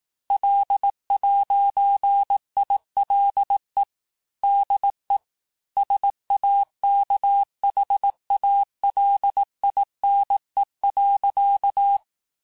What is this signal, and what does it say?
Morse code, heard as a keyed tone, about 18 wpm, 800 Hz. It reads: L'ILE DE SAKHALINE.
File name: texte-clair-vitesse-17mpm-puis-22mpm